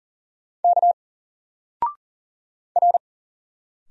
Morse code, heard as a keyed tone, about 39 wpm, 700 Hz. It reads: K R
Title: Single Letter-Number 40wpm